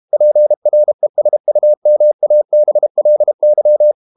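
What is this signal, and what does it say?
Morse code, heard as a keyed tone, about 32 wpm, 600 Hz. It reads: PRESUMABLY
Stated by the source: Morse code practice at 32 wpm from CW med Gnister, reading Ken Follett's novel Never in English.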